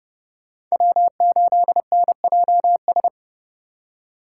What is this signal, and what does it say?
Morse code, heard as a keyed tone, about 29 wpm, 700 Hz. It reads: W8NJS